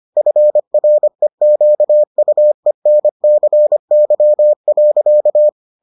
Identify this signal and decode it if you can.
Morse code, heard as a keyed tone, about 25 wpm, 600 Hz. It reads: FREQUENCY.